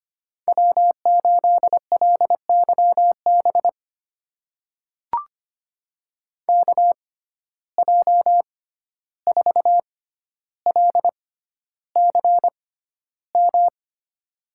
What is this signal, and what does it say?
Morse code, heard as a keyed tone, about 25 wpm, spaced as for 12 wpm, 700 Hz. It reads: W8LYB KJ4LCM
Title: USA Call Signs - 6x Character Spacing - 25wpm